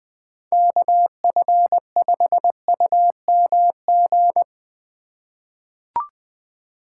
Morse code, heard as a keyed tone, about 20 wpm, 700 Hz. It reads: KF5UMG
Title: USA Call Signs - 6 Characters - 4x Character Spacing - 20wpm